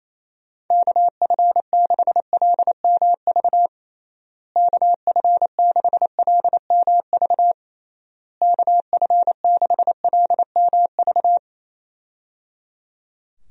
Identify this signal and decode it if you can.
Morse code, heard as a keyed tone, about 28 wpm, 700 Hz. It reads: KF6LMV KF6LMV KF6LMV
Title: USA Call Signs - 6 Characters - Repeated 3x - 28wpm